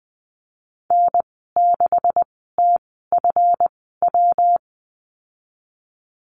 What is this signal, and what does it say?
Morse code, heard as a keyed tone, about 20 wpm, spaced as for 15 wpm, 700 Hz. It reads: N6TFW
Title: 167 Lesson 27g - Number 7 - Call Signs 20wpm